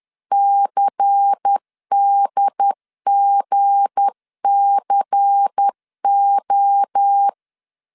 Morse code, heard as a keyed tone, about 10 wpm, 800 Hz. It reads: CDGCO
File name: CW / DET / 1 9 1500C S A 10wpm 4511